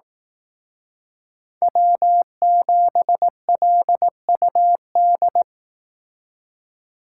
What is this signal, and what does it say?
Morse code, heard as a keyed tone, about 18 wpm, 700 Hz. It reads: W7LUD
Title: Speed Racing - USA 1x3 Call Signs 15wpm